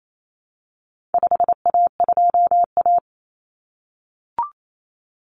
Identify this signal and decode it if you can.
Morse code, heard as a keyed tone, about 28 wpm, 700 Hz. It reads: 5A2A
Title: Common Cut Numbers - 4-Digit 28wpm